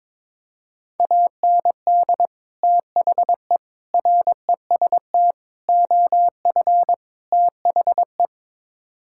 Morse code, heard as a keyed tone, about 22 wpm, 700 Hz. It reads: AND THE REST OF THE
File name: practice / 22wpm / Build A Phrase - 5-Word Phrases - 22wpm